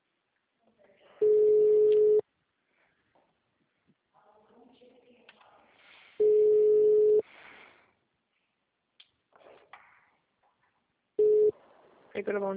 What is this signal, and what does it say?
Ringback tone (Europe, 425 Hz), with speech.